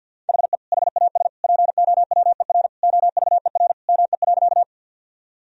Morse code, heard as a keyed tone, about 48 wpm, 700 Hz. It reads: HE HAS POWER OVER ME.
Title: Sentences from Top 1000 Words - Review Part 2 - 50wpm